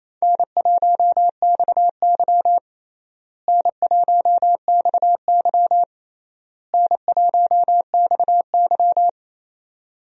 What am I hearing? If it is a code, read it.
Morse code, heard as a keyed tone, about 28 wpm, 700 Hz. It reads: N1XY N1XY N1XY
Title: USA Call Signs - 4 Characters - Repeated 3x - 28wpm